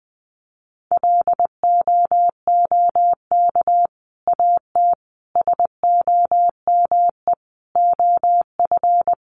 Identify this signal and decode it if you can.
Morse code, heard as a keyed tone, about 20 wpm, 700 Hz. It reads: LOOK AT SOME OF
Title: Sets of 5 Words Encore 20wpm